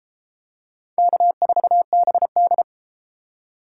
Morse code, heard as a keyed tone, about 31 wpm, 700 Hz. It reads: K4BD